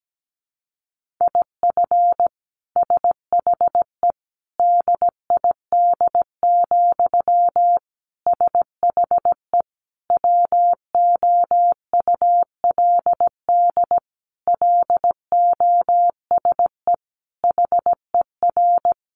Morse code, heard as a keyed tone, about 17 wpm, 700 Hz. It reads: IF SHE DID, SHE WOULD LOSE HER
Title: Sentences from Top 1400 Words 17wpm